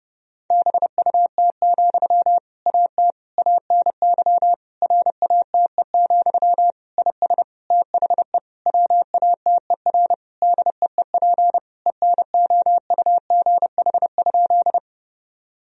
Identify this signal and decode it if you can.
Morse code, heard as a keyed tone, about 30 wpm, 700 Hz. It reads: BUT, AT ANY RATE, IS THE WATER DEEP ENOUGH?
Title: Sentences from Top 1000 Words - Review Part 4 - 30wpm